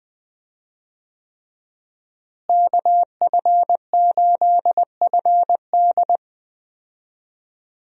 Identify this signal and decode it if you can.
Morse code, heard as a keyed tone, about 20 wpm, 700 Hz. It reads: KF8FD
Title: USA Call Signs - 5 Characters - Repeated 3x - 20wpm